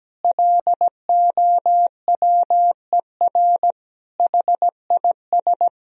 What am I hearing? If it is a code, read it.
Morse code, heard as a keyed tone, about 17 wpm, 700 Hz. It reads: LOWER HIS
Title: Sentences from Top 1000 Words - Review Part 4 - 17wpm